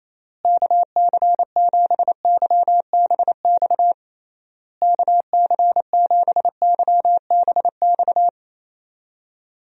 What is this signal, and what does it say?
Morse code, heard as a keyed tone, about 27 wpm, 700 Hz. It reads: KC7YBX KC7YBX
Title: USA Call Signs - 6 Characters - Repeated 3x - 28wpm